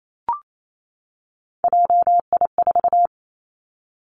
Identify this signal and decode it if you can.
Morse code, heard as a keyed tone, about 28 wpm, 700 Hz. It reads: JI4
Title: Three Challenging Letter-Number Combinations 28wpm